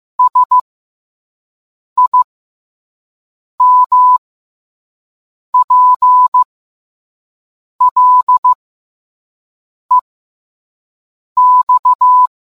Morse code, heard as a keyed tone, about 15 wpm, spaced as for 5 wpm, 1000 Hz. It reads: SIMPLEX